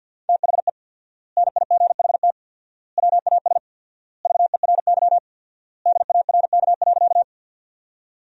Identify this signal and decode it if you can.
Morse code, heard as a keyed tone, about 50 wpm, 700 Hz. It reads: THE NIGHT WAS VERY DARK.
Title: 4x Word-Spacing - Sentences from Top 500 Words - 50wpm